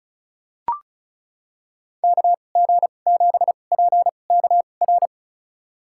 Morse code, heard as a keyed tone, about 36 wpm, spaced as for 25 wpm, 700 Hz. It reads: KG7PKR
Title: USA Call Signs - 6 Characters - 2x Character Spacing - 35wpm